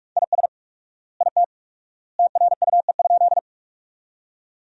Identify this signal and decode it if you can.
Morse code, heard as a keyed tone, about 46 wpm, 700 Hz. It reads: IS IT TRUE?
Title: 4x Word-Spacing - Sentences from Top 400 Words - 45wpm